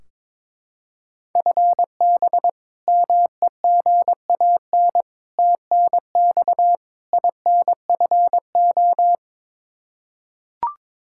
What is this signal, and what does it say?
Morse code, heard as a keyed tone, about 22 wpm, 700 Hz. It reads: FB MEGAN TNX INFO
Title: Practice QSO - Exchange 3 - More Info - 22wpm